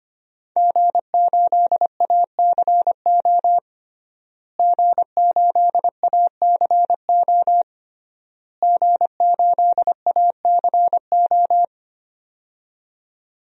Morse code, heard as a keyed tone, about 25 wpm, 700 Hz. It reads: G8ACO G8ACO G8ACO
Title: DXCC-100 Call Signs - 5 Characters - Repeated 3x - 25wpm